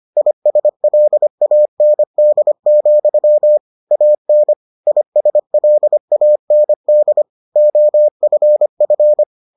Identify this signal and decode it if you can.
Morse code, heard as a keyed tone, about 25 wpm, 600 Hz. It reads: ISLAND, AN ISLAND OFF